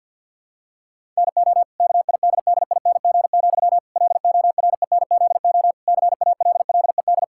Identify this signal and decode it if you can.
Morse code, heard as a keyed tone, about 50 wpm, 700 Hz. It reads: NO KIDDING, LORENZO CALLED
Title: Pangram Challenge v2 - 50wpm